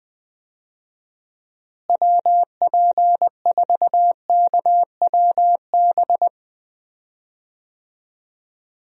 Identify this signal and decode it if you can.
Morse code, heard as a keyed tone, about 20 wpm, 700 Hz. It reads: WP4KWB
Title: USA Call Signs - 6 Characters - Repeated 3x - 20wpm